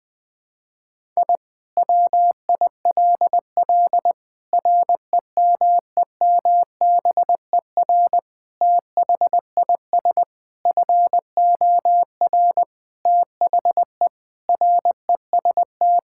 Morse code, heard as a keyed tone, about 20 wpm, 700 Hz. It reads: I WILL REMEMBER THIS FOR THE REST
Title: Sentences with 5-Word Phrases - 20wpm